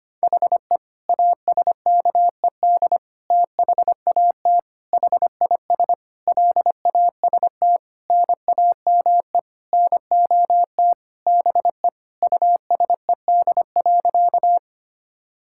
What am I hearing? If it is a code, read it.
Morse code, heard as a keyed tone, about 25 wpm, 700 Hz. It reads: HE ASKED THAT HIS LAST NAME NOT BE USED.
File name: practice / 25wpm / Sentences with 9-Word Phrases - 25wpm